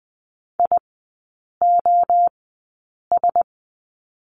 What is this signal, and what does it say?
Morse code, heard as a keyed tone, about 20 wpm, 700 Hz. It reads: I O S
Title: 035 Lesson 7e - Letter s - Three Characters 20wpm